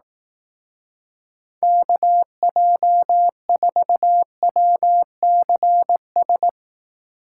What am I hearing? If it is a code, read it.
Morse code, heard as a keyed tone, about 18 wpm, 700 Hz. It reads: KJ4WCS
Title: Speed Racing - USA 2x3 Call Signs 15wpm